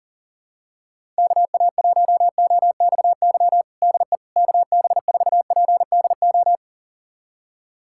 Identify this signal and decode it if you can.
Morse code, heard as a keyed tone, about 38 wpm, 700 Hz. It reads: KA1OXY DE KB4PDO
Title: USA Callsign DE Callsign - 6 Characters - 40wpm